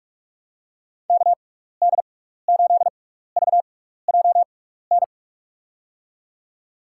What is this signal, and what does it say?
Morse code, heard as a keyed tone, about 42 wpm, spaced as for 15 wpm, 700 Hz. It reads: KD8VJN
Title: USA Call Signs - 6 Characters - 6x Character Spacing - 45wpm